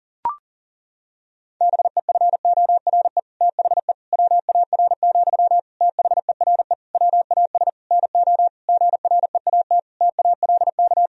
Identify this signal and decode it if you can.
Morse code, heard as a keyed tone, about 40 wpm, 700 Hz. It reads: BEFORE THE WAR, THERE WAS NO GREAT TALK
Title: Sentences from Top 1000 Words - Review Part 2 - 40wpm